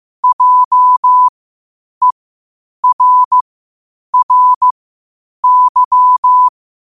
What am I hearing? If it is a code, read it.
Morse code, heard as a keyed tone, about 15 wpm, spaced as for 8 wpm, 1000 Hz. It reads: JERRY
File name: qotdfiles / 8wpm